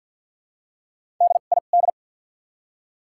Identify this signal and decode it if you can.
Morse code, heard as a keyed tone, about 48 wpm, spaced as for 35 wpm, 700 Hz. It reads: DID
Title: 2x Character Spacing - Top 100 Words - 50wpm